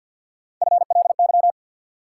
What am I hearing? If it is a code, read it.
Morse code, heard as a keyed tone, about 53 wpm, 700 Hz. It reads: FLY
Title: Sentences from Top 500 Words - New Word Drill - 50wpm